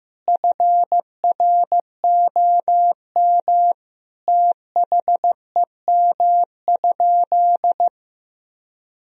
Morse code, heard as a keyed tone, about 14 wpm, 700 Hz. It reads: FROM THEM?
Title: Sentences from Top 1000 Words - Review Part 3 - 15wpm